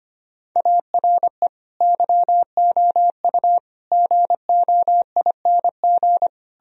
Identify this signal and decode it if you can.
Morse code, heard as a keyed tone, about 30 wpm, 700 Hz. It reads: ARE YOU GOING
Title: Speed-Racing - Sets of 4 Words 25wpm